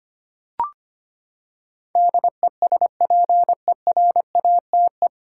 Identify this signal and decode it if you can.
Morse code, heard as a keyed tone, about 25 wpm, 700 Hz. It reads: DESPERATE
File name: Nine Letter Words - 25wpm